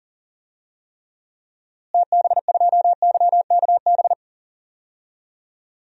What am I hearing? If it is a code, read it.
Morse code, heard as a keyed tone, about 40 wpm, 700 Hz. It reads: TB2YKB
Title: DXCC-100 Call Signs - 6 Characters - Repeated 2x - 40wpm